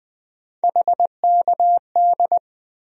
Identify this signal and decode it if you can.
Morse code, heard as a keyed tone, about 20 wpm, 700 Hz. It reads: HKD
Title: USA Call Sign Postfixes - 3 Letters 20wpm